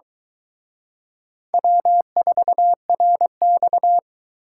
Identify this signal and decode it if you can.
Morse code, heard as a keyed tone, about 24 wpm, 700 Hz. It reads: W4RX